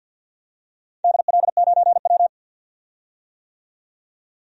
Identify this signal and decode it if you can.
Morse code, heard as a keyed tone, about 46 wpm, 700 Hz. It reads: DL9W
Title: DXCC-100 Call Signs - 4 Characters - 50wpm